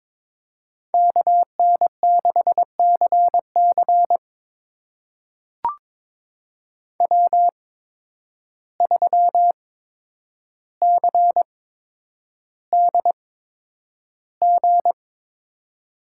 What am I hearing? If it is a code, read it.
Morse code, heard as a keyed tone, about 22 wpm, spaced as for 9 wpm, 700 Hz. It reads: KN6CC W3CDG
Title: USA Call Signs - 5 Characters - 8x Character Spacing - 22wpm